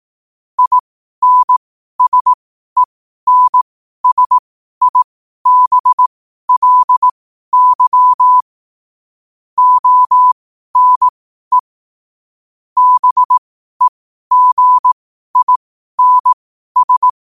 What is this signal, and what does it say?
Morse code, heard as a keyed tone, about 18 wpm, spaced as for 12 wpm, 1000 Hz. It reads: INSENSIBLY ONE BEGINS
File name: qotdfiles / 12wpm